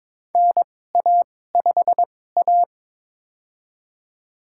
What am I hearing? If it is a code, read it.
Morse code, heard as a keyed tone, about 22 wpm, spaced as for 16 wpm, 700 Hz. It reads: NA5A
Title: USA Call Signs - 4 Characters - 2x Character Spacing - 22wpm